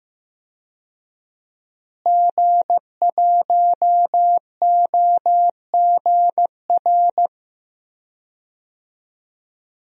Morse code, heard as a keyed tone, about 15 wpm, 700 Hz. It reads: G1OGR
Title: DXCC-100 Call Signs - 5 Characters - Repeated 2x - 15wpm